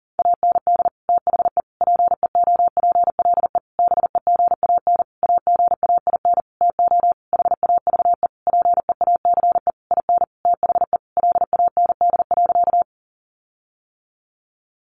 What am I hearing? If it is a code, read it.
Morse code, heard as a keyed tone, about 40 wpm, 700 Hz. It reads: AND THE PEOPLE BEGAN AGAIN TO HAVE PEACE IN THE LAND.
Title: Sentences from Top 1500 Words 40wpm